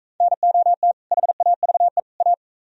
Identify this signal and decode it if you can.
Morse code, heard as a keyed tone, about 43 wpm, 700 Hz. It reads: NOT HAVE A